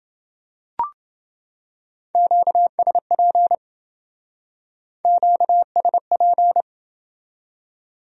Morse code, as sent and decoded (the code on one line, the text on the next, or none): --.- ... .--. / --.- ... .--.
QSP QSP